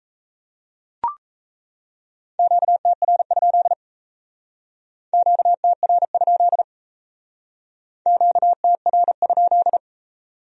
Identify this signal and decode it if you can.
Morse code, heard as a keyed tone, about 36 wpm, 700 Hz. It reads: QTR? QTR? QTR?